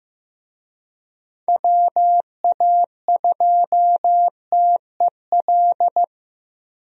Morse code, heard as a keyed tone, about 15 wpm, 700 Hz. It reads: WA2TEL
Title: USA Call Signs - 6 Characters - Repeated 2x - 15wpm